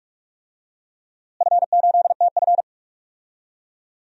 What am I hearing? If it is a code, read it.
Morse code, heard as a keyed tone, about 47 wpm, 700 Hz. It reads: F8TF
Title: DXCC-100 Call Signs - 4 Characters - Repeated 3x - 45wpm